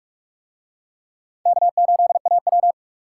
Morse code, heard as a keyed tone, about 44 wpm, 700 Hz. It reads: K8AW